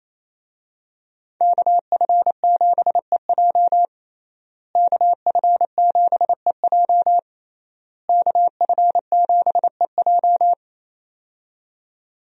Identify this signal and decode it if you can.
Morse code, heard as a keyed tone, about 28 wpm, 700 Hz. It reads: KF7EJ KF7EJ KF7EJ
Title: USA Call Signs - 5 Characters - Repeated 3x - 28wpm